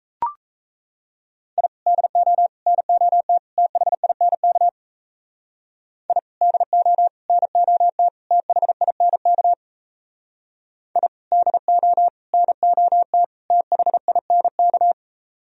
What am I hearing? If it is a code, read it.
Morse code, heard as a keyed tone, about 33 wpm, 700 Hz. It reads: I DO NOT THINK I DO NOT THINK I DO NOT THINK
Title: Speed-Racing - Sets of 4 Words 28wpm